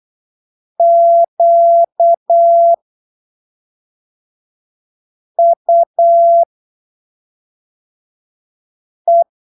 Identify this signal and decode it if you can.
Morse code, heard as a keyed tone, about 8 wpm, spaced as for 3 wpm, 680 Hz. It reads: QUE